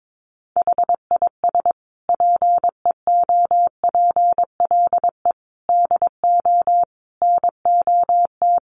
Morse code, heard as a keyed tone, about 22 wpm, 700 Hz. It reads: HIS PEOPLE DO NOT
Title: Sentences from Top 1400 Words 22wpm